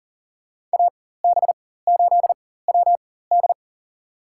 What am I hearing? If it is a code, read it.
Morse code, heard as a keyed tone, about 40 wpm, spaced as for 19 wpm, 700 Hz. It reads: AB8WD
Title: USA Call Signs - 4x Character Spacing - 40wpm